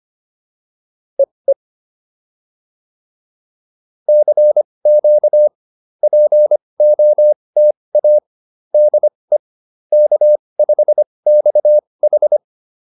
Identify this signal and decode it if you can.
Morse code, heard as a keyed tone, about 25 wpm, spaced as for 20 wpm, 600 Hz. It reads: EE CQ POTA DE K5XH